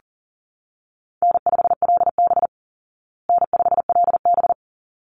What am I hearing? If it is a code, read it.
Morse code, heard as a keyed tone, about 40 wpm, 700 Hz. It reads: N5LB N5LB
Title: Easy Call Signs Repeated 2x - 40wpm